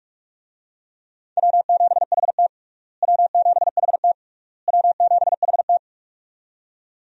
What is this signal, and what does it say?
Morse code, heard as a keyed tone, about 46 wpm, 700 Hz. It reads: W7HT W7HT W7HT
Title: USA Call Signs - 4 Characters - Repeated 3x - 45wpm